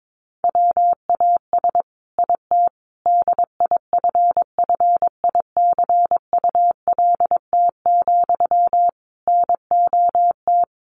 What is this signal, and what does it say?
Morse code, heard as a keyed tone, about 22 wpm, 700 Hz. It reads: WAS IT DIFFICULT, NOT
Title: Sentences from Top 1200 Words 22wpm